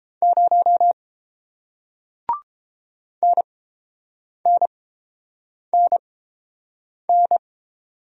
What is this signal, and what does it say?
Morse code, heard as a keyed tone, about 30 wpm, spaced as for 7 wpm, 700 Hz. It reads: 0 NNNN